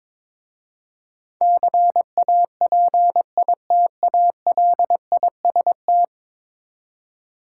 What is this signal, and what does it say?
Morse code, heard as a keyed tone, about 22 wpm, 700 Hz. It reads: CAPITALIST
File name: practice / 22wpm / 10 Letter Words - First 3 Letters Given - 22wpm